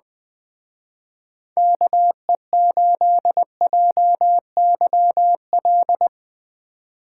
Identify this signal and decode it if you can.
Morse code, heard as a keyed tone, about 20 wpm, 700 Hz. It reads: KE8JYL
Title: Speed Racing - USA Call Signs 17wpm